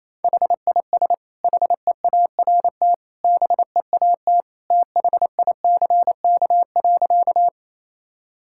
Morse code, heard as a keyed tone, about 27 wpm, 700 Hz. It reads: HIS HEART BEAT THICK.
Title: Sentences from Top 1600 Words 28wpm